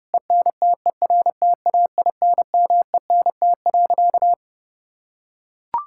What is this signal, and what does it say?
Morse code, heard as a keyed tone, about 30 wpm, 700 Hz. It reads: ENTERTAINMENT.
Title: Sentences from Top 1000 Words - Review Part 6 - 30wpm